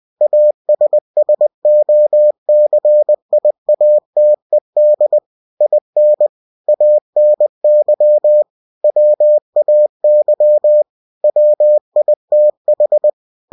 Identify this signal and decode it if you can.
Morse code, heard as a keyed tone, about 20 wpm, 600 Hz. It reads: ASSOCIATED IN ANY WAY WITH
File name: cw-books / winnie-the-pooh / winnie-20wpm-page-0113